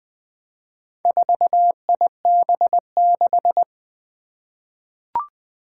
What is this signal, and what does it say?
Morse code, heard as a keyed tone, about 20 wpm, 700 Hz. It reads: 4IB6
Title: Four Challenging Letter-Number Combinations 20wpm